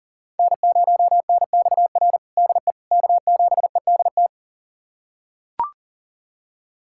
Morse code, heard as a keyed tone, about 40 wpm, 700 Hz. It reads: N0NXR DE K7EDT
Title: USA Callsign DE Callsign - 5 Characters - 40wpm